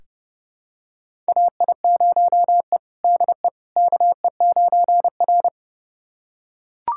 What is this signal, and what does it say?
Morse code, heard as a keyed tone, about 32 wpm, 700 Hz. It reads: AI0E DE KE9R
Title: USA Callsign DE Callsign - 4 Characters - 30wpm